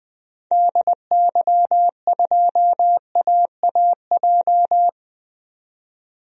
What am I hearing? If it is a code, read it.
Morse code, heard as a keyed tone, about 20 wpm, 700 Hz. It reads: DY2AAJ
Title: DXCC-100 Call Signs - 6 Characters - 20wpm